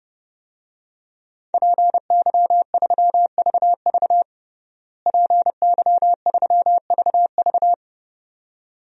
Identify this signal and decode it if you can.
Morse code, heard as a keyed tone, about 30 wpm, 700 Hz. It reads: PY3VV PY3VV
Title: DXCC-100 Call Signs - 5 Characters - Repeated 2x - 30wpm